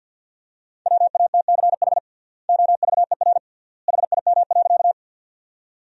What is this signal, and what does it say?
Morse code, heard as a keyed tone, about 53 wpm, 700 Hz. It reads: WATCH OVER HIM.
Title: 3x Word-Spacing - Sentences from Top 300 Words - 50wpm